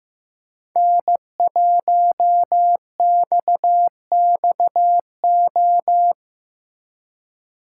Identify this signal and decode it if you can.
Morse code, heard as a keyed tone, about 15 wpm, 700 Hz. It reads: N1XXO
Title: USA Call Signs - 5 Characters - 4x Character Spacing - 15wpm